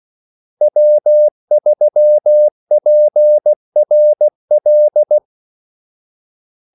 Morse code, heard as a keyed tone, about 16 wpm, 600 Hz. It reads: W3PRL
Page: Kaldesignaler 16 wpm